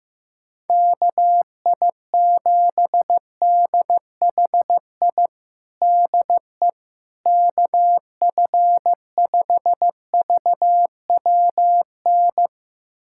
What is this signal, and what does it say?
Morse code, heard as a keyed tone, about 16 wpm, 700 Hz. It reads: KI7DHI DE KF5VWN
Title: USA Callsign DE Callsign - 6 Characters - 15wpm